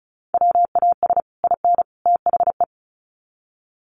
Morse code, heard as a keyed tone, about 34 wpm, 700 Hz. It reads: WAS IN THE
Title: Sets of 3 Words 35wpm